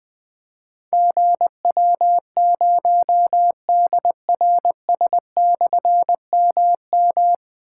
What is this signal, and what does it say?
Morse code, heard as a keyed tone, about 20 wpm, 700 Hz. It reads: GW0DRS/MM
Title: Difficult Call Signs 20wpm